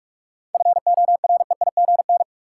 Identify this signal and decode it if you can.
Morse code, heard as a keyed tone, about 46 wpm, 700 Hz. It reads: FOREIGN
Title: Sentences from Top 1400 Words - New Word Drill - 45wpm